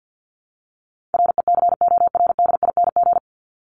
Morse code, heard as a keyed tone, about 52 wpm, 700 Hz. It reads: RECORDING
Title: Top 1501-1750 Words 50wpm